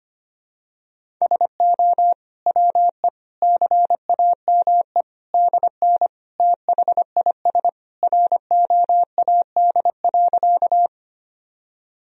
Morse code, heard as a keyed tone, about 25 wpm, 700 Hz. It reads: SO WE CAME DN THIS ROAD.
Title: Sentences from Top 400 Words and Abbreviations 25wpm